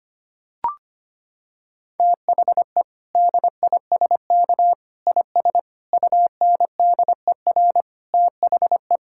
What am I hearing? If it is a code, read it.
Morse code, heard as a keyed tone, about 25 wpm, 700 Hz. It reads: THE DISK IS UNDER THE
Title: Sentences from Top 1000 Words - Review Part 6 - 25wpm